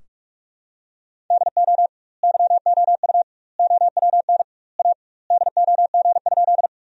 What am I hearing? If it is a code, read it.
Morse code, heard as a keyed tone, about 46 wpm, 700 Hz. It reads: DO YOU OWN A DOG?
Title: 2x Word-Spacing - Sentences from Top 400 Words - 45wpm